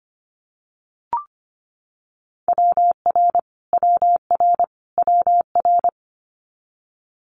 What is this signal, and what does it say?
Morse code, heard as a keyed tone, about 25 wpm, 700 Hz. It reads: WR WR WR
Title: Multi-letter Phonemes Repeated 3x 25wpm